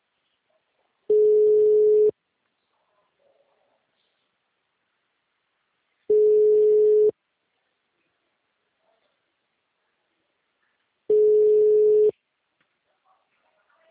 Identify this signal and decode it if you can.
Ringback tone (Europe, 425 Hz).
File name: out- -2012- - - .34